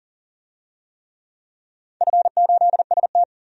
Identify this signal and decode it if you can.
Morse code, heard as a keyed tone, about 40 wpm, 700 Hz. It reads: F8ST
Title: DXCC-100 Call Signs - 4 Characters - 40wpm